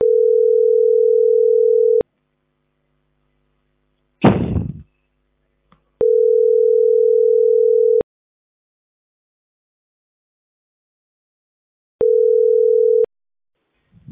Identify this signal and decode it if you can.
Ringback tone (North America), 440 + 480 Hz.